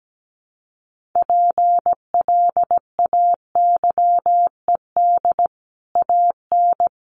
Morse code, heard as a keyed tone, about 17 wpm, 700 Hz. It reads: PLAYED AN
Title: Sets of 5 Words Encore2 17wpm